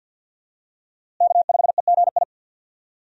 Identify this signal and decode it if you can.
Morse code, heard as a keyed tone, about 51 wpm, 700 Hz. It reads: K5EGI
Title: USA Call Signs - 5 Characters - Repeated 2x - 50wpm